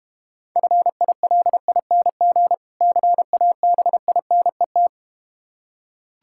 Morse code, heard as a keyed tone, about 32 wpm, 700 Hz. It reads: FILING CABINET